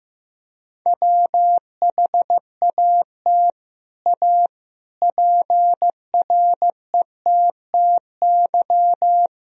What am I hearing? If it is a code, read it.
Morse code, heard as a keyed tone, about 15 wpm, 700 Hz. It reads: WHAT A PRETTY
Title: Sentences from Top 1000 Words - Review Part 1 - 15wpm